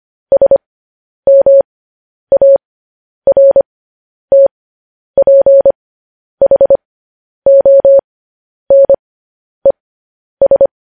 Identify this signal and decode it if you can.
Morse code, heard as a keyed tone, about 25 wpm, spaced as for 10 wpm, 550 Hz. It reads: SMARTPHONES
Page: news headlines in morse code at 10 wpm